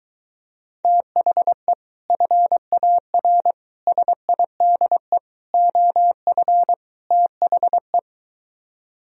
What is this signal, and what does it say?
Morse code, heard as a keyed tone, about 23 wpm, 700 Hz. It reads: THE FAR SIDE OF THE